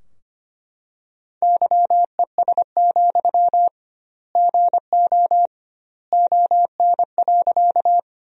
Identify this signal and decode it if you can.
Morse code, heard as a keyed tone, about 25 wpm, 700 Hz. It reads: YES, GO ON.
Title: 2x Word-Spacing - Sentences from Top 500 Words - 25wpm